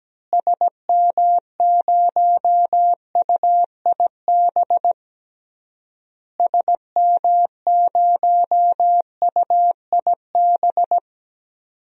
Morse code, heard as a keyed tone, about 17 wpm, 700 Hz. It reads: SM0UIB SM0UIB
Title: DXCC-100 Call Signs - 6 Characters - Repeated 3x - 17wpm